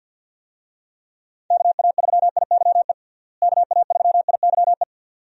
Morse code, heard as a keyed tone, about 51 wpm, 700 Hz. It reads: KA3IYE KA3IYE
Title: USA Call Signs - 6 Characters - Repeated 2x - 50wpm